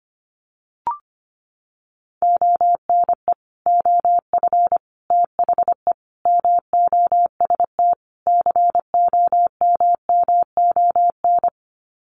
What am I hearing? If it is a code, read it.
Morse code, heard as a keyed tone, about 25 wpm, 700 Hz. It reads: ONE OF THE MOST COMMON
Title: Sets of 5 Words Encore2 25wpm